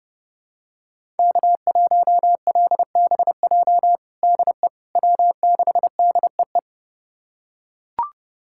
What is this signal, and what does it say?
Morse code, heard as a keyed tone, about 30 wpm, 700 Hz. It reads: K1LBJ DE W6DEE
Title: USA Callsign DE Callsign - 5 Characters - 30wpm